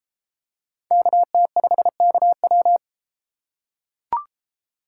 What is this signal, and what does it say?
Morse code, heard as a keyed tone, about 32 wpm, 700 Hz. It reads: KT5KW